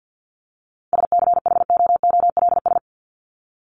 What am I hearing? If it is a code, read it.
Morse code, heard as a keyed tone, about 50 wpm, 700 Hz. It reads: SCHOOLS
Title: ES:S Suffix - Top 100 Words with Base Words 50wpm